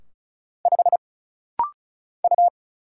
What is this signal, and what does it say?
Morse code, heard as a keyed tone, about 34 wpm, 700 Hz. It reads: 5 U
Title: Single Letter-Number - Rapid-Fire 35wpm